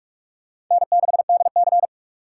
Morse code, heard as a keyed tone, about 43 wpm, 700 Hz. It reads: N6DC